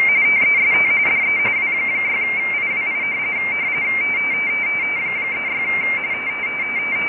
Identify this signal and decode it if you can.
RTTY, about 45.45 baud. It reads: TEST DE W6IWI 36
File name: rtty / websdr recording 14086.0kHz